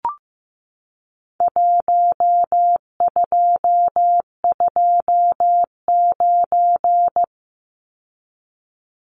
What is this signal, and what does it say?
Morse code, heard as a keyed tone, about 15 wpm, 700 Hz. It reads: 1229
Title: Numbers - Contest Style - 4-Digit 15wpm